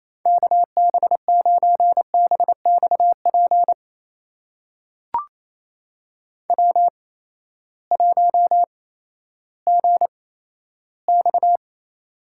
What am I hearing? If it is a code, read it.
Morse code, heard as a keyed tone, about 28 wpm, spaced as for 14 wpm, 700 Hz. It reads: KB9BXP W1GX
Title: USA Call Signs - 8x Character Spacing - 28wpm